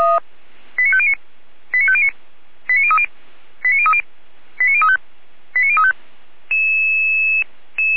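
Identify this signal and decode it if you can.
Five-tone selective call (ZVEI), heard as quick runs of five tones, tones of about 70 ms. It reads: 89509 89509 89030 89030 89035 89035
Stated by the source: ZVEI five-tone selective call received on an SDR, archived by Feuerwehr Orlamünde.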